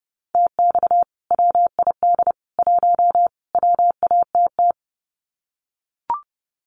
Morse code, heard as a keyed tone, about 30 wpm, 700 Hz. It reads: TX WID 1 WATT
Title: Common QSO Phrases 30wpm